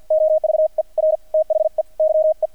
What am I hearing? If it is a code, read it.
Morse code, heard as a keyed tone, about 48 wpm, spaced as for 42 wpm, 630 Hz. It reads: OVER THE YE
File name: archived / CRSnet